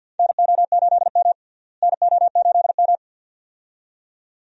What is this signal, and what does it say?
Morse code, heard as a keyed tone, about 47 wpm, 700 Hz. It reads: NO8M NO8M
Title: USA Call Signs - 4 Characters - Repeated 2x - 50wpm